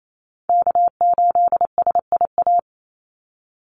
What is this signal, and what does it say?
Morse code, heard as a keyed tone, about 28 wpm, 700 Hz. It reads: K8SIA
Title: Call Signs Repeated 3x - 28wpm